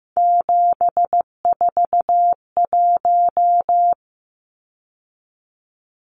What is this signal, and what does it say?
Morse code, heard as a keyed tone, about 15 wpm, 700 Hz. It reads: 741
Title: Numbers - 3-Digit 15wpm